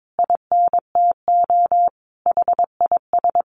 Morse code, heard as a keyed tone, about 22 wpm, 700 Hz. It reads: INTO HIS
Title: Sentences from Top 1300 Words 22wpm